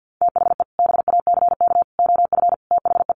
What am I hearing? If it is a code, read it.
Morse code, heard as a keyed tone, about 50 wpm, 700 Hz. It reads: THE BACK OF THE
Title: Sets of 5 Words Encore2 50wpm